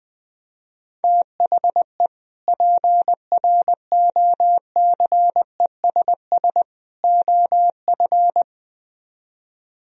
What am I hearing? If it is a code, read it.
Morse code, heard as a keyed tone, about 20 wpm, 700 Hz. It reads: THE PROCESS OF
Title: Speed-Racing - Sets of 3 Words 20wpm